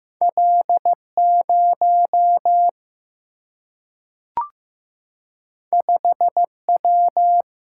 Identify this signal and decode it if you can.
Morse code, heard as a keyed tone, about 15 wpm, 700 Hz. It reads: L0 5W
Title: Single-Letter Single-Number 15wpm